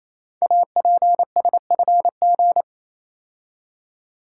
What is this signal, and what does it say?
Morse code, heard as a keyed tone, about 27 wpm, 700 Hz. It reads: APSFG
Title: Code Groups - 5 Letters - 28wpm